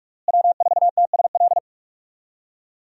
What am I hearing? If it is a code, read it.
Morse code, heard as a keyed tone, about 41 wpm, 700 Hz. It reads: W4TSL
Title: USA Call Signs - 5 Characters - Repeated 2x - 45wpm